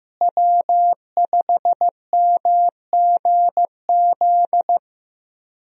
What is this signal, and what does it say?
Morse code, heard as a keyed tone, about 15 wpm, 700 Hz. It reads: W5MGZ
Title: USA Call Signs - 5 Characters - Repeated 2x - 15wpm